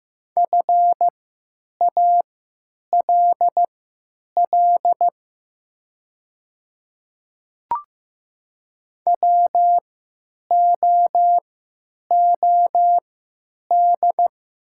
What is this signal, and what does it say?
Morse code, heard as a keyed tone, about 15 wpm, spaced as for 8 wpm, 700 Hz. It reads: FALL WOOD
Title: 3x Character Spacing - Top 500 Words - 15wpm